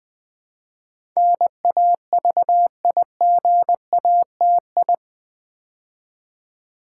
Morse code, heard as a keyed tone, about 21 wpm, 700 Hz. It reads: NAVIGATI
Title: Build A Word - 4 Letters At A Time - 10-Letter Words - 20wpm